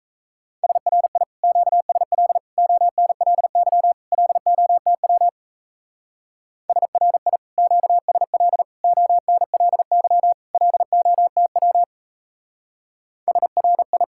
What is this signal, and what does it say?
Morse code, heard as a keyed tone, about 34 wpm, 700 Hz. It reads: SRI QSL ONLY LOTW SRI QSL ONLY LOTW SRI
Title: Speed-Racing Common QSO Phrases Encore - 28wpm